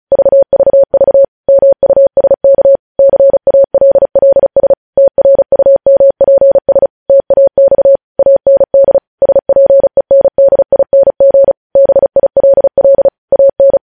Morse code, heard as a keyed tone, about 36 wpm, 550 Hz. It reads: VVV MUSK CALLS TRUMPS TAX AND SPENDING BILL AN
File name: podcast / News headlines-1x-35x35